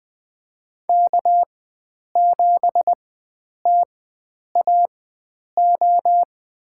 Morse code, heard as a keyed tone, about 20 wpm, spaced as for 9 wpm, 700 Hz. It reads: K7TAO
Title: USA Call Signs - 5 Characters - 4x Character Spacing - 20wpm